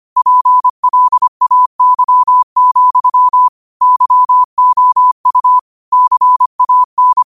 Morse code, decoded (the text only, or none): PLAY, YOU CAN